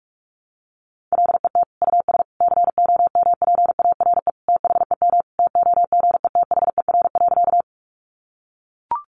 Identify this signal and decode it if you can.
Morse code, heard as a keyed tone, about 44 wpm, 700 Hz. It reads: LET US COMPARE THEM TOGETHER.
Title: Sentences from Top 1300 Words 45wpm